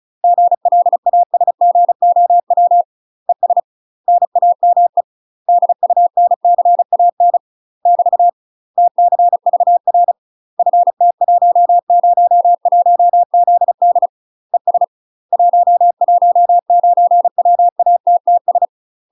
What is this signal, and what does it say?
Morse code, heard as a keyed tone, about 36 wpm, 700 Hz. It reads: GLASGOW ES NAME DUNCAN <BT> TCVR FT101ZD ES 119WATTS
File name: UK MASTER QSOs TRAINING / 35WPM / UK MASTER QSOs 0016